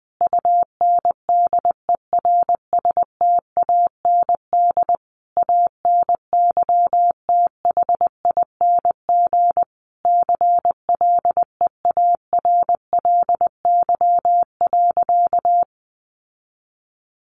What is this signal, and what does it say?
Morse code, heard as a keyed tone, about 20 wpm, 700 Hz. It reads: UNDERSTAND ANYTHING CLEARLY.
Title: Sentences from Top 1300 Words 20wpm